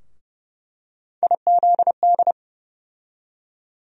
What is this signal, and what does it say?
Morse code, heard as a keyed tone, about 30 wpm, 700 Hz. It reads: IZD
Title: USA Call Sign Postfixes - 3 Letters 30wpm